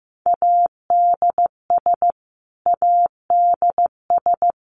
Morse code, heard as a keyed tone, about 15 wpm, 700 Hz. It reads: ADS ADS
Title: Abbreviations Repeated 3x - 15wpm